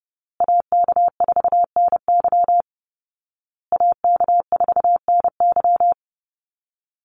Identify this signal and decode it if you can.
Morse code, heard as a keyed tone, about 30 wpm, 700 Hz. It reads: AK4NY AK4NY
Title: Call Signs Repeated 2x - 30wpm